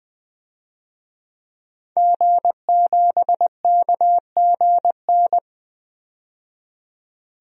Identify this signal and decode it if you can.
Morse code, heard as a keyed tone, about 20 wpm, 700 Hz. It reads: G7KGN
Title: DXCC-100 Call Signs - 5 Characters - 20wpm